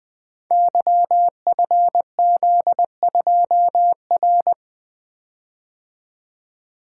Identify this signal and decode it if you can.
Morse code, heard as a keyed tone, about 20 wpm, 700 Hz. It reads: YFZ2R
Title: Code Groups - 5 Letter:Numbers - 20wpm